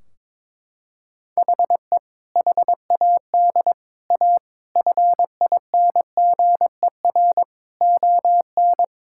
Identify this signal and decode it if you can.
Morse code, heard as a keyed tone, about 21 wpm, 700 Hz. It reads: HE HAD A FINGER ON
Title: Sentences from Top 1600 Words 22wpm